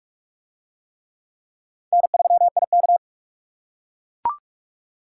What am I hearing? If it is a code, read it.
Morse code, heard as a keyed tone, about 44 wpm, 700 Hz. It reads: N3IK